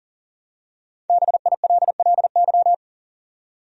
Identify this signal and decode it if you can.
Morse code, heard as a keyed tone, about 40 wpm, 700 Hz. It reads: BILLY
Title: USA Common Names - 40wpm